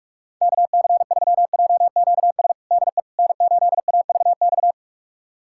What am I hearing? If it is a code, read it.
Morse code, heard as a keyed tone, about 41 wpm, 700 Hz. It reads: KC3JQS DE N8AVX
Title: USA Callsign DE Callsign 45wpm